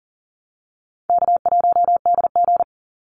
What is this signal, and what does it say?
Morse code, heard as a keyed tone, about 40 wpm, 700 Hz. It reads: K1DG
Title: Easy Call Signs - 40wpm